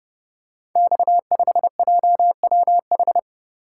Morse code, heard as a keyed tone, about 30 wpm, 700 Hz. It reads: X5JWH
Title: Code Groups - 5 Letter:Numbers - 30wpm